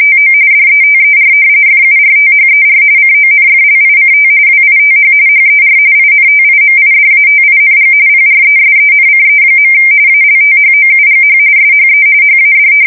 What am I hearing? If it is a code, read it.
RTTY, about 45.45 baud. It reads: TO DETER FURTHER CONFLICT WITH HER INTIMIDATING PRESENCE. ALTHOUGH SHE HAD
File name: fsk170 / battleshipnj